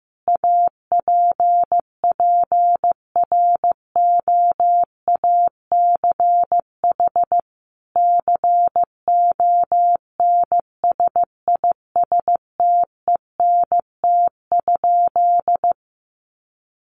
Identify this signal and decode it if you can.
Morse code, heard as a keyed tone, about 15 wpm, 700 Hz. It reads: APPROACH CONSISTENT?
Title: Sentences from Top 1500 Words 15wpm